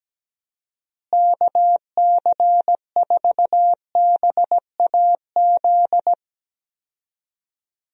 Morse code, heard as a keyed tone, about 17 wpm, 700 Hz. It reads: KC4BAZ
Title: USA Call Signs - 6 Characters - Repeated 3x - 17wpm